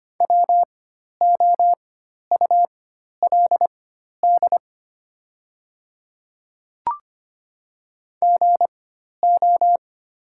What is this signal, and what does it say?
Morse code, heard as a keyed tone, about 25 wpm, spaced as for 12 wpm, 700 Hz. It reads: WOULD GO
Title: 4x Character Spacing - Top 100 Words - 25wpm